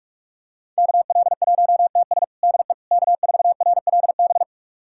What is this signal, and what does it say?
Morse code, heard as a keyed tone, about 46 wpm, 700 Hz. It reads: KL1TS DE K4RLB
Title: USA Callsign DE Callsign 45wpm